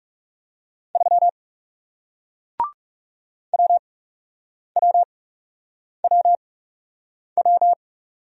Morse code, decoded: 3 WWWW

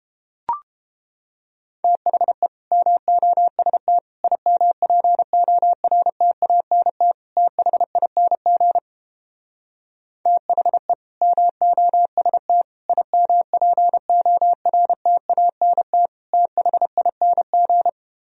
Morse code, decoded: THE MOST IMPORTANT THING THE MOST IMPORTANT THING

